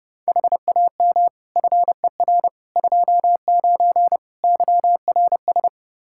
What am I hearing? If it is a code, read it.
Morse code, heard as a keyed tone, about 30 wpm, 700 Hz. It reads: HAM FER 29 YRS
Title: K1USN SST QSO Practice 30wpm